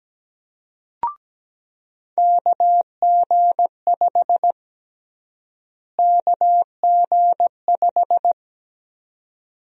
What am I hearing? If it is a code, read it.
Morse code, heard as a keyed tone, about 17 wpm, 700 Hz. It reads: KG5 KG5
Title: USA Call Sign Prefixes - Repeated 2x - 17wpm